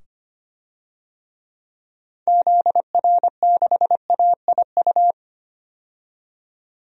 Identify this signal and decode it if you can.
Morse code, heard as a keyed tone, about 25 wpm, 700 Hz. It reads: ZR6AIU
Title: DXCC-100 Call Signs - 6 Characters - 25wpm